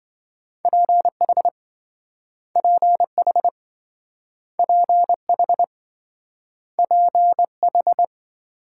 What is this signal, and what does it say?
Morse code, heard as a keyed tone, about 27 wpm, 700 Hz. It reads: PH PH PH PH